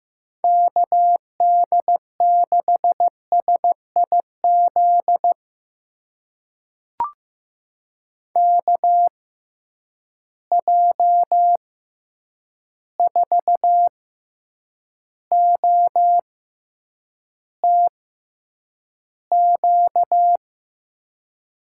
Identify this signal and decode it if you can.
Morse code, heard as a keyed tone, about 15 wpm, spaced as for 8 wpm, 700 Hz. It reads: KD6SIZ KJ4OTQ
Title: USA Call Signs - 6 Characters - 6x Character Spacing - 15wpm